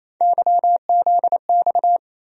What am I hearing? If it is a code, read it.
Morse code, heard as a keyed tone, about 28 wpm, 700 Hz. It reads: YZX
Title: USA Call Sign Postfixes - 3 Letters 28wpm